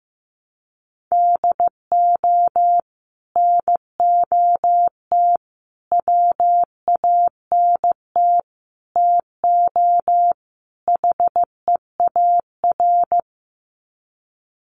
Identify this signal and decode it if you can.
Morse code, heard as a keyed tone, about 15 wpm, 700 Hz. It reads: DO NOT WANT TO HEAR
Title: Sets of 5 Words 15wpm